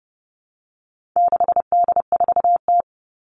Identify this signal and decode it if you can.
Morse code, heard as a keyed tone, about 30 wpm, 700 Hz. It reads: 6D4T
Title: All Cut Numbers - 4-Digit 30wpm